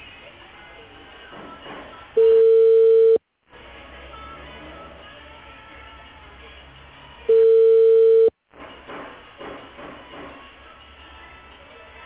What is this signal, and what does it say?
Ringback tone (China, 450 Hz).